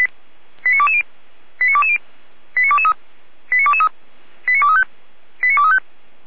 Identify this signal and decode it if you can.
Five-tone selective call (ZVEI), heard as quick runs of five tones, tones of about 70 ms. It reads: 89220 89220 89223 89223 89235 89235